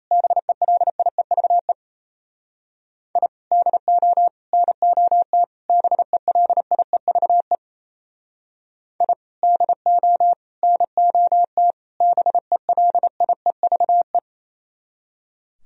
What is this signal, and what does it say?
Morse code, heard as a keyed tone, about 37 wpm, 700 Hz. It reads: BELIEVE I DO NOT BELIEVE I DO NOT BELIEVE